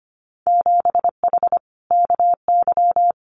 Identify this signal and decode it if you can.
Morse code, heard as a keyed tone, about 25 wpm, 700 Hz. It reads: 7H KY
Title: Winter Field Day Exchanges Encore 25wpm